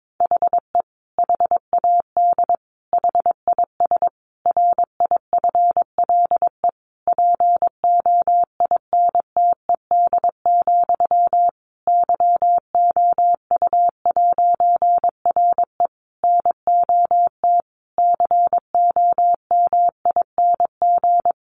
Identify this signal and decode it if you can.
Morse code, heard as a keyed tone, about 22 wpm, 700 Hz. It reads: HE HAD HIS RIFLE POINTED, YOU'RE NOT COMING
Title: A Sound Of Thunder 22wpm